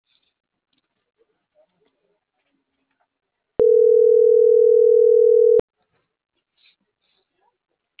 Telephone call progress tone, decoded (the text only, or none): ringback tone